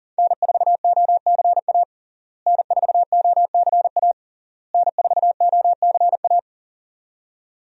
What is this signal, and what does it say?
Morse code, heard as a keyed tone, about 40 wpm, 700 Hz. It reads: N4OCA N4OCA N4OCA
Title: USA Call Signs - 5 Characters - Repeated 3x - 40wpm